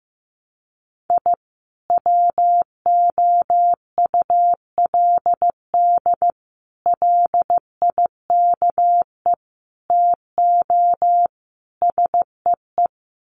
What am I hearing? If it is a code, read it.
Morse code, heard as a keyed tone, about 15 wpm, 700 Hz. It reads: I WOULD LIKE TO SEE
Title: Sets of 5 Words 15wpm